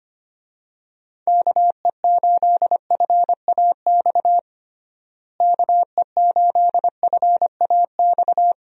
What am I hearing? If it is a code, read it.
Morse code, heard as a keyed tone, about 25 wpm, 700 Hz. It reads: KE8FAX KE8FAX
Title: USA Call Signs - 6 Characters - Repeated 3x - 25wpm